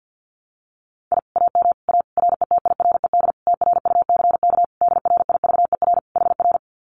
Morse code, heard as a keyed tone, about 50 wpm, 700 Hz. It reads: I AM A RETIRED TRUCK DRIVER HR
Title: Common QSO Phrases Encore 50wpm